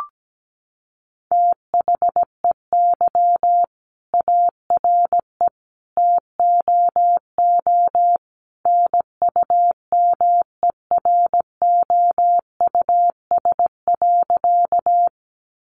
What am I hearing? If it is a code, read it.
Morse code, heard as a keyed tone, about 17 wpm, 700 Hz. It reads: THEY ARE TOO NUMEROUS.
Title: Sentences from Top 1400 Words 17wpm